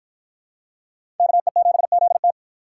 Morse code, heard as a keyed tone, about 50 wpm, 700 Hz. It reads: KE7ZT